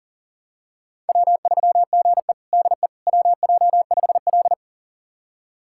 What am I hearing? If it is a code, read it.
Morse code, heard as a keyed tone, about 37 wpm, 700 Hz. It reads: W3GE DE WJ5L
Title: USA Callsign DE Callsign - 4 Characters - 40wpm